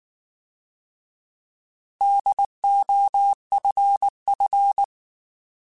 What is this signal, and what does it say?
Morse code, heard as a keyed tone, about 19 wpm, 800 Hz. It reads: DOFF